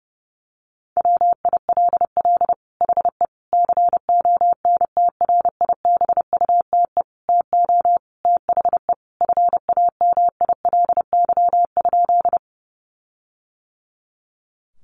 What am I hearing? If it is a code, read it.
Morse code, heard as a keyed tone, about 30 wpm, 700 Hz. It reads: WILL HE CONTRIBUTE TO THE FAMILY?
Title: Sentences from Top 1400 Words 30wpm